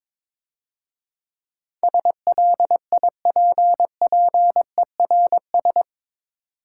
Morse code, heard as a keyed tone, about 24 wpm, 700 Hz. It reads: SLIPPERS
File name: Easy Words - Speed-Racing with Hints - 22wpm